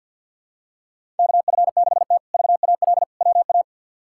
Morse code, heard as a keyed tone, about 53 wpm, 700 Hz. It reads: KF6T VAL WA
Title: K1USN SST QSO Practice - 50wpm